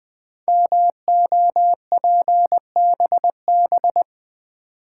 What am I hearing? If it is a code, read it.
Morse code, heard as a keyed tone, about 20 wpm, 700 Hz. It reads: MOPBB
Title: Code Groups - 5 Letters - 20wpm